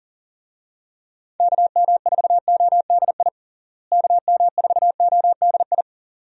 Morse code, heard as a keyed tone, about 40 wpm, 700 Hz. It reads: KM4ODI KM4ODI
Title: USA Call Signs - 6 Characters - Repeated 2x - 40wpm